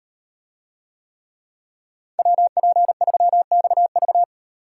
Morse code, heard as a keyed tone, about 36 wpm, 700 Hz. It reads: WP3XV